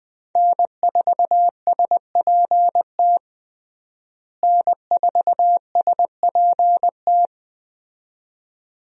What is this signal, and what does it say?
Morse code, heard as a keyed tone, about 21 wpm, 700 Hz. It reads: N4SPT N4SPT
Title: USA Call Signs - 5 Characters - Repeated 2x - 20wpm